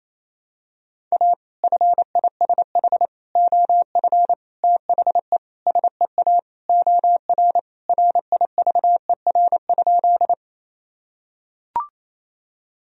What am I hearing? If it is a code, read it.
Morse code, heard as a keyed tone, about 28 wpm, 700 Hz. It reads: A FISH OF THE SEA OR RIVER?
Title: Sentences from Top 1000 Words - Review Part 1 - 28wpm